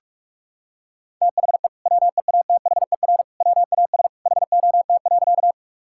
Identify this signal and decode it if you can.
Morse code, heard as a keyed tone, about 41 wpm, 700 Hz. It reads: THE WEATHER WAS HOT.
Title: Sentences from Top 1000 Words - Review Part 5 - 45wpm